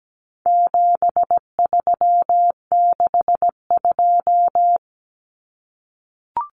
Morse code, heard as a keyed tone, about 17 wpm, 700 Hz. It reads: 7362
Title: Numbers - 4-Digit 17wpm